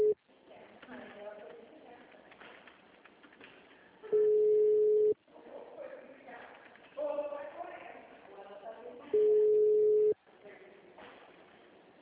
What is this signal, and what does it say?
Ringback tone (Europe, 425 Hz), with speech.
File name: out- -2004- - - .70